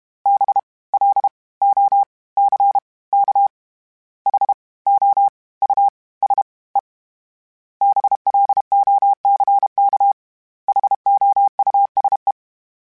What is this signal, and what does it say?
Morse code, heard as a keyed tone, about 32 wpm, 800 Hz. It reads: B L O C K H O U S E BLOCK HOUSE
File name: General / General-mp3s / Houses @30